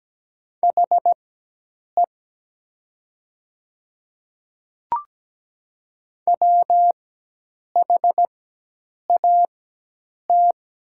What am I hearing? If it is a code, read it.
Morse code, heard as a keyed tone, about 17 wpm, spaced as for 8 wpm, 700 Hz. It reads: HE WHAT
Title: 4x Character Spacing - Top 200 Words - 17wpm